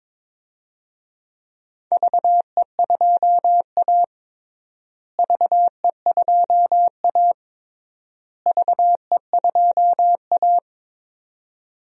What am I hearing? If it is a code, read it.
Morse code, heard as a keyed tone, about 22 wpm, 700 Hz. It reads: VE2A VE2A VE2A
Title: DXCC-100 Call Signs - 4 Characters - Repeated 3x - 22wpm